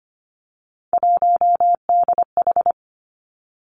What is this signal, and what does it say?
Morse code, heard as a keyed tone, about 25 wpm, 700 Hz. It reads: 1DH
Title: Three Challenging Letter-Number Combinations 25wpm